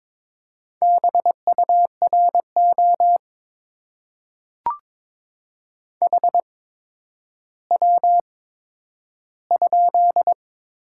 Morse code, heard as a keyed tone, about 22 wpm, spaced as for 11 wpm, 700 Hz. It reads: BURO HW?
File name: Top 90 QSO Elements - 8x Character Spacing - 22wpm